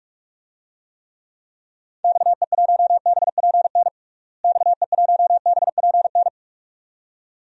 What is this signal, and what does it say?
Morse code, heard as a keyed tone, about 42 wpm, 690 Hz. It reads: XE1BPN XE1BPN
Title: DXCC-100 Call Signs - 6 Characters - Repeated 2x - 45wpm